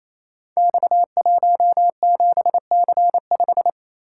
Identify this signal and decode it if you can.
Morse code, heard as a keyed tone, about 28 wpm, 700 Hz. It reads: X17C5
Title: Code Groups - 5 Letter:Numbers - 28wpm